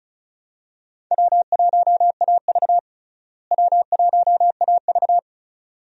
Morse code, heard as a keyed tone, about 34 wpm, 700 Hz. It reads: W1AV W1AV
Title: USA Call Signs - 4 Characters - Repeated 2x - 35wpm